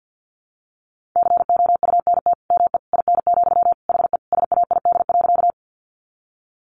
Morse code, heard as a keyed tone, about 50 wpm, 700 Hz. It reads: COUNT ME IN, HE SAID.
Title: Sentences from Top 1500 Words 50wpm